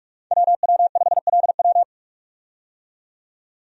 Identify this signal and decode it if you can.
Morse code, heard as a keyed tone, about 41 wpm, 700 Hz. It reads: WW5LW
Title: USA Call Signs - 5 Characters - 45wpm